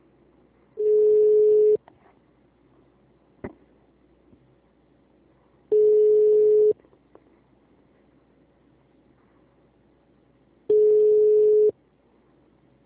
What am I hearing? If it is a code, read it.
Ringback tone (Europe, 425 Hz), with speech.
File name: out- -2006- - - .278